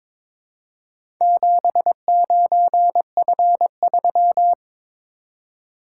Morse code, heard as a keyed tone, about 22 wpm, 700 Hz. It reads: 79F3
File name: practice / 22wpm / Four Letter-Number Combinations - 22wpm